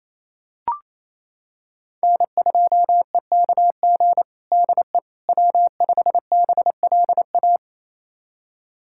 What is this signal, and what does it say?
Morse code, heard as a keyed tone, about 28 wpm, 700 Hz. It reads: N2EKG DE W5BLA
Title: USA Callsign DE Callsign - 5 Characters - 28wpm